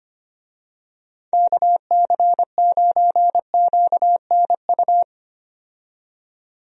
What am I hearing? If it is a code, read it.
Morse code, heard as a keyed tone, about 25 wpm, 700 Hz. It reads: KC9QNU
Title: USA Call Signs - 6 Characters - 25wpm